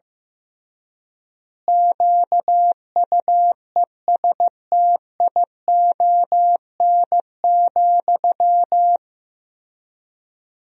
Morse code, heard as a keyed tone, about 15 wpm, 700 Hz. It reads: QUESTION,
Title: 3x Word-Spacing - Sentences from Top 400 Words - 15wpm